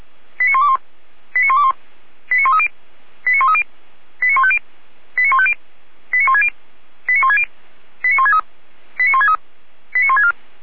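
Five-tone selective call (ZVEI), heard as quick runs of five tones, tones of about 70 ms. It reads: 89121 89121 89130 89130 89150 89150 89160 89160 89163 89163 89164
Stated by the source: ZVEI five-tone selective call 89140 received via SDR, fire brigade Orlamünde (feuerwehr-orlamuende) audio archive.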